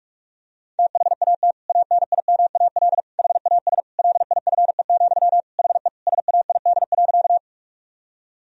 Morse code, heard as a keyed tone, about 46 wpm, 700 Hz. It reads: THAT ANIMAL HAS LIFE, HE SAID.
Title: Sentences from Top 1000 Words - Review Part 1 - 45wpm